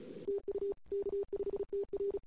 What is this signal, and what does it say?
Morse code, heard as a keyed tone, about 34 wpm, 410 Hz. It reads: TU K5TR
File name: CW / 2008-ss-cw-k5tr-op-n5rz -21:00